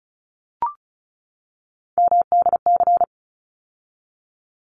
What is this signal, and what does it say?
Morse code, heard as a keyed tone, about 34 wpm, 700 Hz. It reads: MDC
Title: ARRL Sections 35wpm